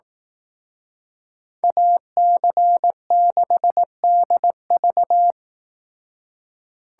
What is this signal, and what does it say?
Morse code, heard as a keyed tone, about 18 wpm, 700 Hz. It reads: AC6DV
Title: Speed Racing - USA 2x2 Call Signs 15wpm